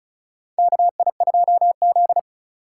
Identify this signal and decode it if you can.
Morse code, heard as a keyed tone, about 34 wpm, 700 Hz. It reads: KI2Z
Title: USA Call Signs - 4 Characters - 3x Character Spacing - 35wpm